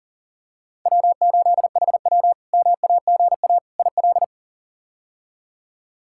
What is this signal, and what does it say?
Morse code, heard as a keyed tone, about 37 wpm, 700 Hz. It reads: W8HW MAGA IL
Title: K1USN SST QSO Practice - 40wpm